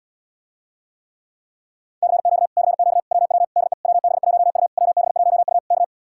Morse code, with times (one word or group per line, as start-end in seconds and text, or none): 2.0-2.5 CQ
2.6-3.0 CQ
3.1-3.4 WW
3.6-3.7 DE
3.8-4.7 WB1U
4.8-5.6 WB1U
5.7-5.8 K